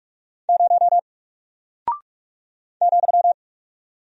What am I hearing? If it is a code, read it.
Morse code, heard as a keyed tone, about 44 wpm, 700 Hz. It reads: 0 ,
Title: Single Character (Letter:Number:Punctuation:Prosign) 45wpm